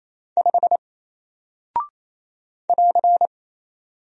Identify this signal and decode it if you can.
Morse code, heard as a keyed tone, about 28 wpm, 700 Hz. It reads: 5 <AR>
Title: Single Character (Letter:Number:Punctuation:Prosign) Alt 28wpm